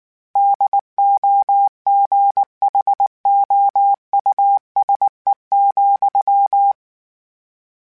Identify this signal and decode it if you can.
Morse code, heard as a keyed tone, about 19 wpm, 800 Hz. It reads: DOGHOUSE,
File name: General / General-mp3s / Houses @18